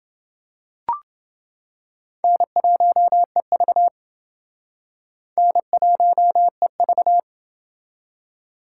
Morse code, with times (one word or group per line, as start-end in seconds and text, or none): 2.2-3.9 N1EV
5.4-7.2 N1EV